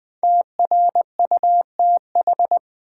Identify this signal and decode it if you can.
Morse code, heard as a keyed tone, about 20 wpm, 700 Hz. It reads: TRUTH
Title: Sentences from Top 800 Words - New Word Drill - 20wpm